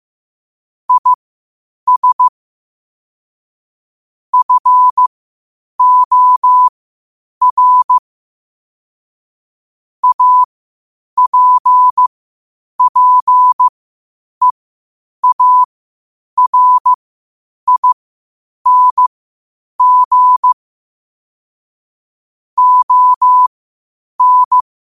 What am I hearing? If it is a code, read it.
Morse code, heard as a keyed tone, about 15 wpm, spaced as for 8 wpm, 1000 Hz. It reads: IS FOR APPEARING ON
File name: qotdfiles / 8wpm